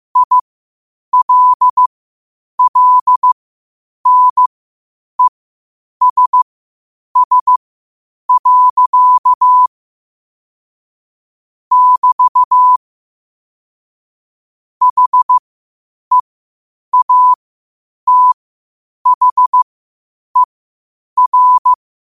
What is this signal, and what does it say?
Morse code, heard as a keyed tone, about 15 wpm, spaced as for 8 wpm, 1000 Hz. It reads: ILLNESS. <BT> HEATHER
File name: qotdfiles / 8wpm